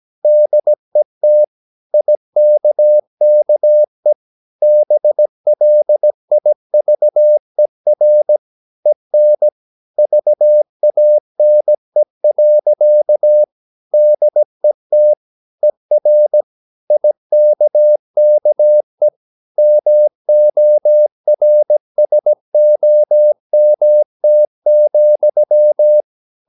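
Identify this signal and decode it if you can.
Morse code, heard as a keyed tone, about 17 wpm, 600 Hz. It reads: DET IKKE BLIVER EN VANE. DET ER IKKE MORSOMT,
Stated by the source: Slow (QRS) Morse practice at 17 wpm, from CW med Gnister.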